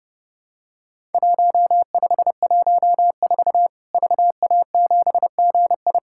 Morse code, heard as a keyed tone, about 30 wpm, 700 Hz. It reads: 1514 VA7GI
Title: Worked All Europe DX Contest - QTC Exchange - 30wpm